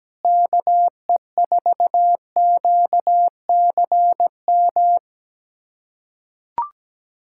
Morse code, heard as a keyed tone, about 17 wpm, 700 Hz. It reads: KE4QCM
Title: USA Callsign DE Callsign 17wpm